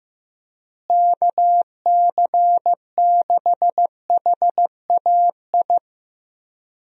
Morse code, heard as a keyed tone, about 15 wpm, 700 Hz. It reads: KC6HAI
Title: USA Call Signs - 6 Characters - Repeated 3x - 15wpm